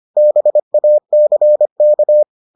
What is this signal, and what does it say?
Morse code, heard as a keyed tone, about 25 wpm, 600 Hz. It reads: BACK